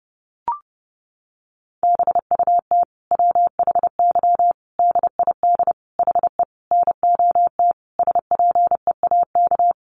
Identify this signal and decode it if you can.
Morse code, heard as a keyed tone, about 30 wpm, 700 Hz. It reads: BUT WHY DID HE NOT SPEAK
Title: Sentences from Top 1300 Words 30wpm